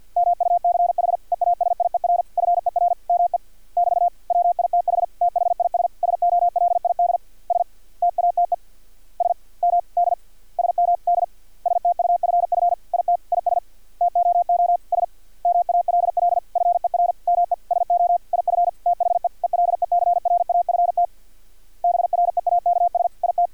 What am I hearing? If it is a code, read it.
Morse code, heard as a keyed tone, about 50 wpm, 700 Hz. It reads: MUCH EASIER FER ME <BT> WITH THIS SOLID STATE SMD SMD STUFF IT IS TOO SMALL FER ME SO IF THE ELECRAFT BREAKS IT
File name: archived / CRSnet